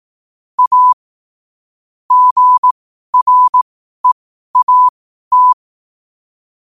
Morse code, heard as a keyed tone, about 18 wpm, spaced as for 12 wpm, 1000 Hz. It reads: A GREAT
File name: qotdfiles / 12wpm